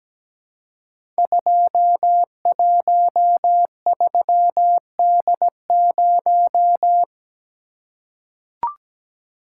Morse code, heard as a keyed tone, about 17 wpm, 700 Hz. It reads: 213D0
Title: Code Groups - 5 Letter:Numbers - 17wpm